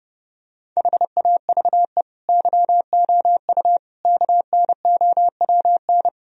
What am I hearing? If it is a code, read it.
Morse code, heard as a keyed tone, about 30 wpm, 700 Hz. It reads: HAVE YOU KNOWN
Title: Sentences from Top 1000 Words - Review Part 3 - 30wpm